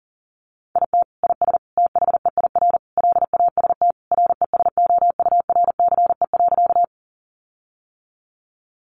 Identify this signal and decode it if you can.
Morse code, heard as a keyed tone, about 40 wpm, 700 Hz. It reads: IT IS THEIR LAST RESOURCE.
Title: Sentences from Top 1300 Words 40wpm